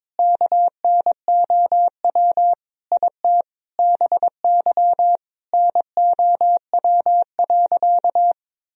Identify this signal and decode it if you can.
Morse code, heard as a keyed tone, about 22 wpm, 700 Hz. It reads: KNOW IT BY NOW.
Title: Sentences from Top 1000 Words - Review Part 2 - 22wpm